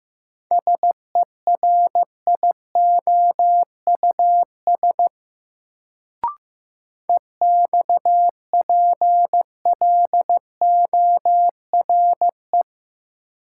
Morse code, heard as a keyed tone, about 15 wpm, 700 Hz. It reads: SERIOUS EXPLORE
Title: Sentences from Top 1100 Words - New Word Drill Alt - 15wpm